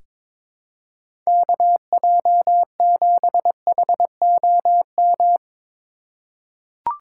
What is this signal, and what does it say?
Morse code, heard as a keyed tone, about 22 wpm, 700 Hz. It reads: KJ7HOM
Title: USA Call Signs - 4x Character Spacing - 22wpm